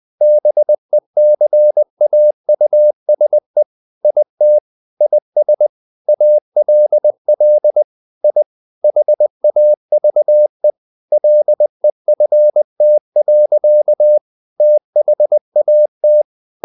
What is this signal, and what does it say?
Morse code, heard as a keyed tone, about 20 wpm, 600 Hz. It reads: BECAUSE IT IS ALL I HAVE LEFT. THAT
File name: the old man and the sea part 2 20wpm